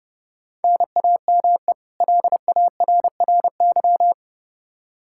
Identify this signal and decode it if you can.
Morse code, heard as a keyed tone, about 32 wpm, 700 Hz. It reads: NAME LARRY
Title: Practice QSO - Exchange 1 - Beginning - 30wpm